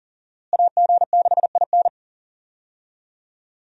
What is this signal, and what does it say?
Morse code, heard as a keyed tone, about 40 wpm, 700 Hz. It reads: AG6IN
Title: USA Call Signs - 5 Characters - Repeated 3x - 40wpm